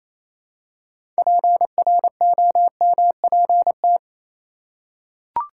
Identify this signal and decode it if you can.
Morse code, heard as a keyed tone, about 27 wpm, 700 Hz. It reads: PROMPT
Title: Six Letter Words - 28wpm